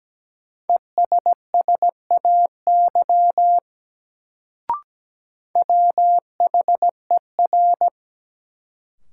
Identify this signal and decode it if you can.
Morse code, heard as a keyed tone, about 17 wpm, 700 Hz. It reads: ESSAY WHER
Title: Sentences from Top 1500 Words - New Word Drill - 17wpm